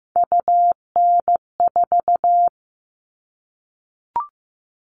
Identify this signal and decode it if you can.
Morse code, heard as a keyed tone, about 15 wpm, 700 Hz. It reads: UN4
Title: All Cut Numbers - 3-Digit 15wpm